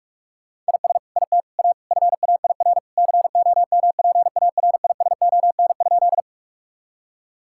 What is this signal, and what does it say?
Morse code, heard as a keyed tone, about 46 wpm, 700 Hz. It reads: IS IT A FAIR COMPARISON?
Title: Sentences from Top 1600 Words 45wpm